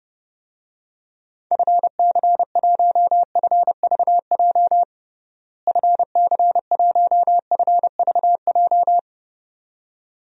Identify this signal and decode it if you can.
Morse code, heard as a keyed tone, about 32 wpm, 700 Hz. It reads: FC1FVJ FC1FVJ
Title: DXCC-100 Call Signs - 6 Characters - Repeated 2x - 30wpm